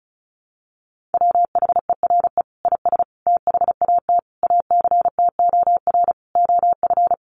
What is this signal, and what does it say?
Morse code, heard as a keyed tone, about 34 wpm, 700 Hz. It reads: WHERE IS THAT ACTOR OF
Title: Sentences from Top 1500 Words 35wpm